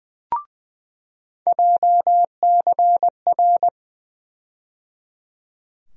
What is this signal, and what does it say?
Morse code, heard as a keyed tone, about 20 wpm, 700 Hz. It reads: JCR
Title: USA Call Sign Postfixes - 3 Letters 20wpm